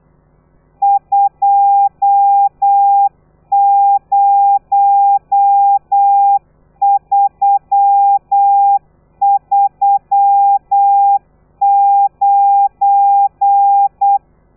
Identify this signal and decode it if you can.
Morse code, heard as a keyed tone, about 8 wpm, 800 Hz. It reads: 20339
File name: telegrafia / i2tfj / Lez12 35